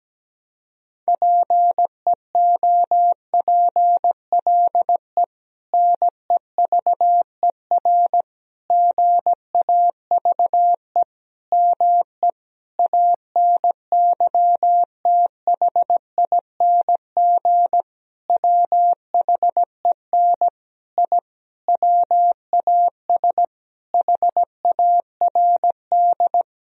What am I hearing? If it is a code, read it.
Morse code, heard as a keyed tone, about 17 wpm, 700 Hz. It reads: PEOPLE NEVER GAVE ME ANYTHING WHEN I WAS HARD
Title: Sentences from Top 1000 Words - Review Part 1 - 17wpm